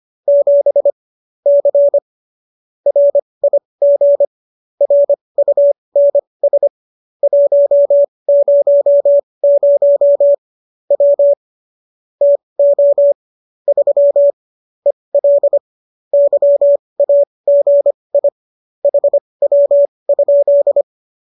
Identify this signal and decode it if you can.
Morse code, heard as a keyed tone, about 25 wpm, spaced as for 20 wpm, 570 Hz. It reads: 7 C RIG RUNS 100 W TO 3 EL YAGI HW?
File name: qso208 20